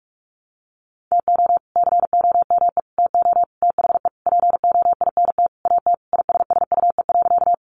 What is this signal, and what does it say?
Morse code, heard as a keyed tone, about 44 wpm, 700 Hz. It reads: TO COME TO THE POINT AT ISSUE.
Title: Sentences from Top 900 Words 45wpm